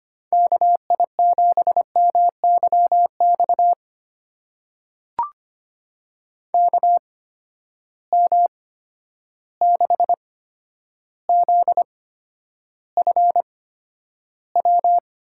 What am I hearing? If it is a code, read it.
Morse code, heard as a keyed tone, about 25 wpm, spaced as for 11 wpm, 700 Hz. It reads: KI7MYX KM6ZFW
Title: USA Call Signs - 6 Characters - 8x Character Spacing - 25wpm